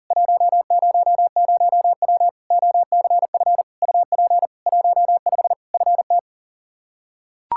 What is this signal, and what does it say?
Morse code, heard as a keyed tone, about 40 wpm, 700 Hz. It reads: 100W OCF UP 15 FT
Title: K1USN SST QSO Practice 40wpm